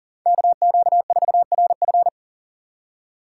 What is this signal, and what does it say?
Morse code, heard as a keyed tone, about 40 wpm, 700 Hz. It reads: KQ4RF
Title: USA Call Signs - 5 Characters - 4x Character Spacing - 40wpm